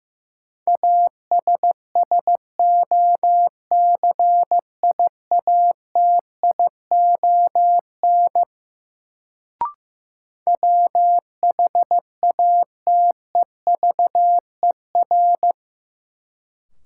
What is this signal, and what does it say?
Morse code, heard as a keyed tone, about 15 wpm, 700 Hz. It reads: ASSOCIATION WHATEVER
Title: Sentences from Top 1100 Words - New Word Drill Alt - 15wpm